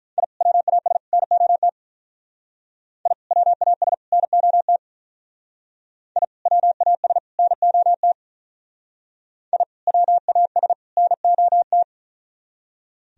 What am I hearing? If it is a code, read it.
Morse code, heard as a keyed tone, about 43 wpm, 700 Hz. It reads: I WAS NOT I WAS NOT I WAS NOT I WAS NOT
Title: Speed-Racing - Sets of 3 Words 35wpm